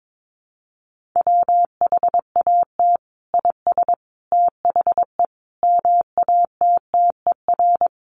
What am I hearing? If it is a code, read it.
Morse code, heard as a keyed tone, about 22 wpm, 700 Hz. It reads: WHAT IS THE MATTER
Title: Sentences from Top 700 Words 22wpm